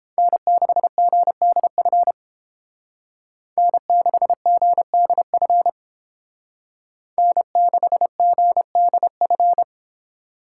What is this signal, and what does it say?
Morse code, heard as a keyed tone, about 31 wpm, 700 Hz. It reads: N6GDF N6GDF N6GDF